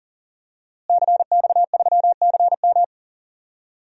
Morse code, heard as a keyed tone, about 37 wpm, 700 Hz. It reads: CX3CM
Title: DXCC-100 Call Signs - 5 Characters - Repeated 3x - 40wpm